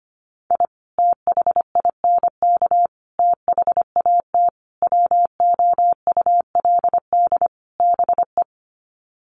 Morse code, heard as a keyed tone, about 25 wpm, 700 Hz. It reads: I THINK THAT WOULD BE
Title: Sets of 5 Words Encore2 25wpm